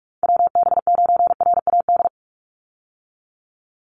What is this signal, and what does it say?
Morse code, heard as a keyed tone, about 44 wpm, 700 Hz. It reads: WB9RAD
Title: USA Call Signs 45wpm